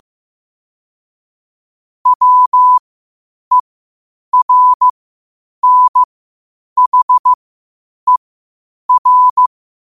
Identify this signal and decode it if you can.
Morse code, heard as a keyed tone, about 15 wpm, spaced as for 8 wpm, 1000 Hz. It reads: WERNHER
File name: 8wpm